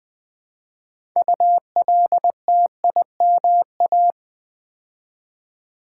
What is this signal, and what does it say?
Morse code, heard as a keyed tone, about 20 wpm, 700 Hz. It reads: ULTIMA
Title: Build A Word - 2 Letters At A Time - 8-Letter Words - 20wpm